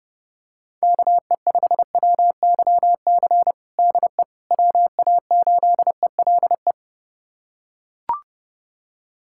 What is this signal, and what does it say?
Morse code, heard as a keyed tone, about 30 wpm, 700 Hz. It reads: KE5WYC DE WA8ELE
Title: USA Callsign DE Callsign - 6 Characters - 30wpm